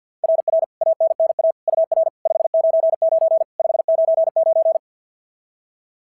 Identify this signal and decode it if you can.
Morse code, heard as a keyed tone, about 50 wpm, 690 Hz. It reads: RR ANNA UR 599 599
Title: SOTA QSO Practice 50wpm